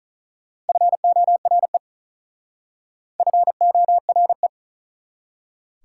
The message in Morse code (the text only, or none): FORE FORE